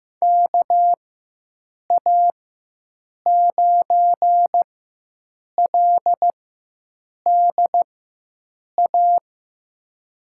Morse code, heard as a keyed tone, about 15 wpm, spaced as for 7 wpm, 700 Hz. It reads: KA9LDA
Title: USA Call Signs - 4x Character Spacing - 15wpm